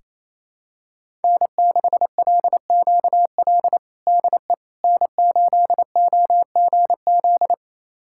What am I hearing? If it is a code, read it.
Morse code, heard as a keyed tone, about 28 wpm, 700 Hz. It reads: N6LQL DE N8OGZ
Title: USA Callsign DE Callsign - 5 Characters - 28wpm